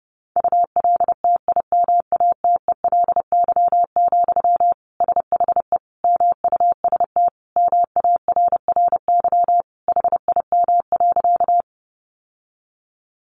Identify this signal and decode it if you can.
Morse code, heard as a keyed tone, about 30 wpm, 700 Hz. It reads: ULTIMATELY, SHE MUST MARRY HIM.
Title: Sentences from Top 1200 Words 30wpm